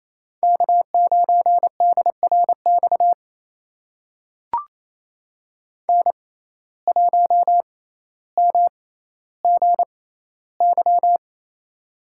Morse code, heard as a keyed tone, about 28 wpm, spaced as for 14 wpm, 700 Hz. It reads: K9DRX N1MGY
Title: USA Call Signs - 5 Characters - 6x Character Spacing - 28wpm